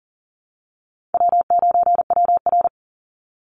Morse code, heard as a keyed tone, about 42 wpm, 700 Hz. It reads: W9WR
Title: Easy Call Signs - 40wpm